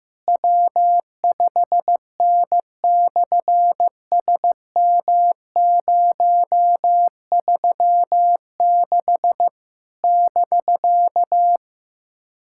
Morse code, heard as a keyed tone, about 15 wpm, 700 Hz. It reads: W5N/SM036 <BK>
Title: SOTA QSO Practice 15wpm